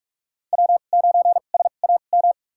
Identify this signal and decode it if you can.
Morse code, heard as a keyed tone, about 46 wpm, spaced as for 33 wpm, 700 Hz. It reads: W9SAM
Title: USA Call Signs - 5 Characters - 2x Character Spacing - 45wpm